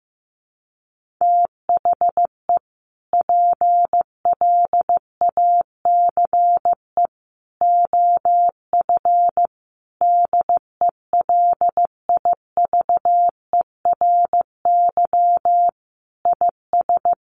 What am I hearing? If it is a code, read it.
Morse code, heard as a keyed tone, about 14 wpm, 700 Hz. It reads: THE PLACE OF DELIVERY IS
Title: Sentences from Top 1500 Words 15wpm